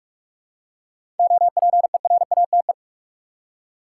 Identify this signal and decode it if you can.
Morse code, heard as a keyed tone, about 45 wpm, 700 Hz. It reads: OPERATE
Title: Seven Letter Words - 45wpm